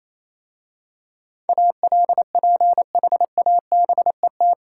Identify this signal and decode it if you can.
Morse code, heard as a keyed tone, about 26 wpm, 690 Hz. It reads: ALPHABET
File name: Easy Words - Speed-Racing with Hints - 22wpm